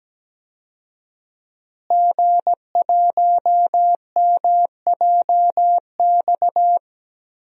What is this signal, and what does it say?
Morse code, heard as a keyed tone, about 17 wpm, 700 Hz. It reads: G1MJX
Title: DXCC-100 Call Signs - 5 Characters - Repeated 3x - 17wpm